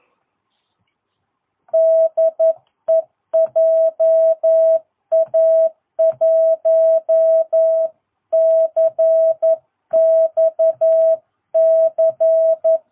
Morse code, heard as a keyed tone, about 11 wpm, 660 Hz. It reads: DE JA1CXC
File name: jh8cbh / 2022cw